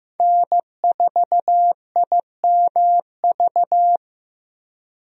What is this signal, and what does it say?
Morse code, heard as a keyed tone, about 15 wpm, 700 Hz. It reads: N4IMV
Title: USA Call Signs - 5 Characters - Repeated 2x - 15wpm